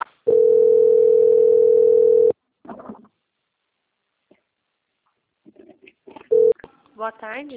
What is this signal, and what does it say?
Ringback tone (North America), 440 + 480 Hz, with speech.
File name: out- - - - - .7970